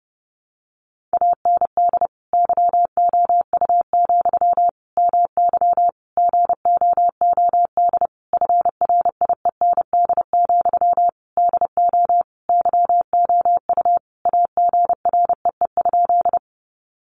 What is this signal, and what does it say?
Morse code, heard as a keyed tone, about 30 wpm, 700 Hz. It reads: AND YOU, MY GOOD FRIEND, DO YOU AGREE?
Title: Sentences from Top 1100 Words 30wpm